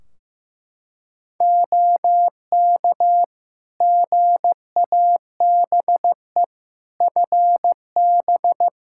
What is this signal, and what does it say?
Morse code, heard as a keyed tone, about 15 wpm, 700 Hz. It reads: OK GABE FB
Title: Practice QSO - Exchange 2 - Info - 15wpm